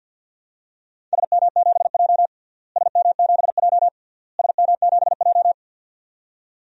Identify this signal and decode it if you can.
Morse code, heard as a keyed tone, about 52 wpm, 700 Hz. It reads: SM7J SM7J SM7J
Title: DXCC-100 Call Signs - 4 Characters - Repeated 3x - 50wpm